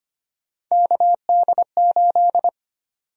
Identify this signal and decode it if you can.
Morse code, heard as a keyed tone, about 25 wpm, 700 Hz. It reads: KD8
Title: USA Call Sign Prefixes - Repeated 2x - 25wpm